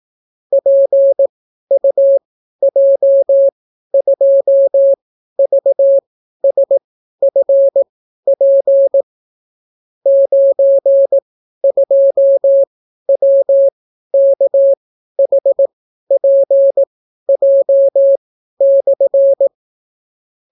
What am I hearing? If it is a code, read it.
Morse code, heard as a keyed tone, about 18 wpm, spaced as for 12 wpm, 550 Hz. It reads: PUJ2VSFP 92WKHPJ/